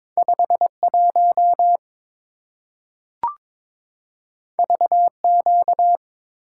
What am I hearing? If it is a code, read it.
Morse code, heard as a keyed tone, about 22 wpm, 700 Hz. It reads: 51 VQ
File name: practice / 22wpm / Two Letter-Number Combinations - 22wpm